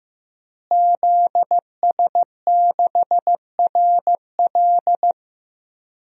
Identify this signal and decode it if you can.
Morse code, heard as a keyed tone, about 15 wpm, 700 Hz. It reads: ZS6RL
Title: DXCC-100 Call Signs - 5 Characters - Repeated 3x - 15wpm